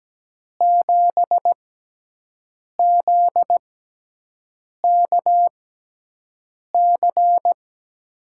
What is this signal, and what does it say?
Morse code, heard as a keyed tone, about 17 wpm, spaced as for 6 wpm, 700 Hz. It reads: 7ZKC
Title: USA Call Signs - 5 Characters - 6x Character Spacing - 17wpm